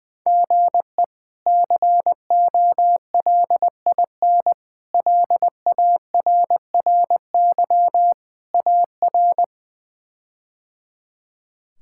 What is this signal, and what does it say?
Morse code, heard as a keyed tone, about 20 wpm, 700 Hz. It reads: GE COLIN LARRY AR
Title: K1USN SST QSO Practice - 20wpm